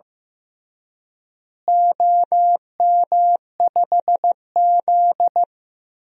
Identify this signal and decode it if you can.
Morse code, heard as a keyed tone, about 15 wpm, 700 Hz. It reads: OM5Z
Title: DXCC-100 Call Signs - 4 Characters - Repeated 3x - 15wpm